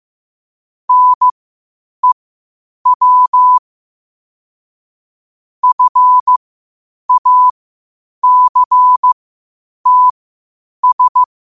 Morse code, heard as a keyed tone, about 15 wpm, spaced as for 8 wpm, 1000 Hz. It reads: NEW FACTS
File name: qotdfiles / 8wpm